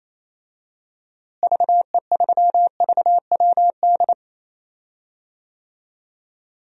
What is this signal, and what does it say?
Morse code, heard as a keyed tone, about 28 wpm, 700 Hz. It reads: VE3VWD
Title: DXCC-100 Call Signs - 6 Characters - Repeated 3x - 28wpm